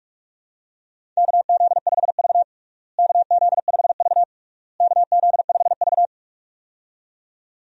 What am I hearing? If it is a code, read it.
Morse code, heard as a keyed tone, about 46 wpm, 700 Hz. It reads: KZ5V KZ5V KZ5V
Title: USA Call Signs - 4 Characters - Repeated 3x - 45wpm